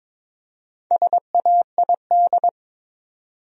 Morse code, heard as a keyed tone, about 22 wpm, 700 Hz. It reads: SAID
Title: Top 100 Words Repeated 22wpm